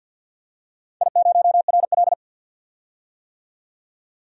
Morse code, heard as a keyed tone, about 52 wpm, 700 Hz. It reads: I0RL
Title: DXCC-100 Call Signs - 4 Characters - 50wpm